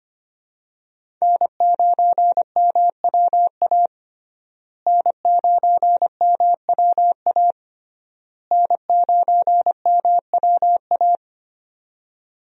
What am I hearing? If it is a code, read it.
Morse code, heard as a keyed tone, about 25 wpm, 700 Hz. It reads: N9MWA N9MWA N9MWA
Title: USA Call Signs - 5 Characters - Repeated 3x - 25wpm